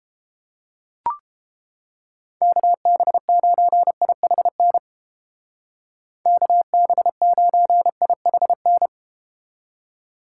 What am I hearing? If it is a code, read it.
Morse code, heard as a keyed tone, about 32 wpm, 700 Hz. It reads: KB9IHN KB9IHN